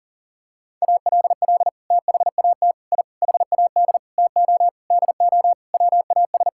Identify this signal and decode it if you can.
Morse code, heard as a keyed tone, about 40 wpm, 700 Hz. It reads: ALL THAT I HAD TO DO WAS
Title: Sentences from Top 1000 Words - Review Part 6 - 40wpm